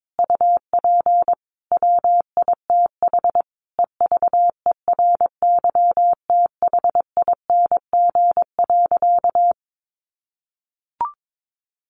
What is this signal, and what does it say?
Morse code, heard as a keyed tone, about 22 wpm, 700 Hz. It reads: UP WITH EVERYTHING.
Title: Sentences from Top 1100 Words 22wpm